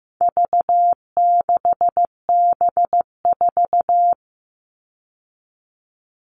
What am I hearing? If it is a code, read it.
Morse code, heard as a keyed tone, about 15 wpm, 700 Hz. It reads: V6B4
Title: All Cut Numbers - 4-Digit 15wpm